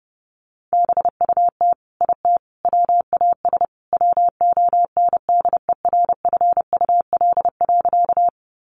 Morse code, heard as a keyed tone, about 30 wpm, 700 Hz. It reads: BUT IT WAS WONDERFUL.
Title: Sentences from Top 1100 Words 30wpm